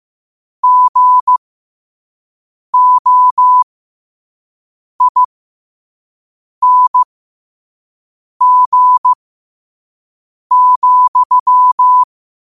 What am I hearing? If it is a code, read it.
Morse code, heard as a keyed tone, about 15 wpm, spaced as for 5 wpm, 1000 Hz. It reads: GOING,